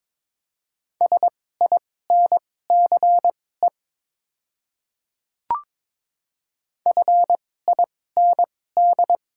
Morse code, heard as a keyed tone, about 22 wpm, spaced as for 16 wpm, 700 Hz. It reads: SINCE FIND
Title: 2x Character Spacing - Top 500 Words - 22wpm